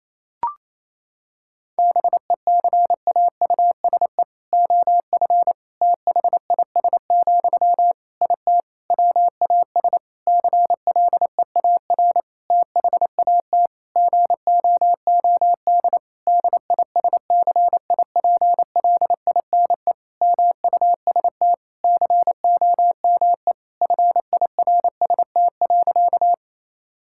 Morse code, heard as a keyed tone, about 28 wpm, 700 Hz. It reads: BECAUSE OF THIS, IT WAS CLEAR THAT GOOD DISCIPLINE MUST COME FIRST.
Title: Sentences from Top 1000 Words - Review Part 6 - 28wpm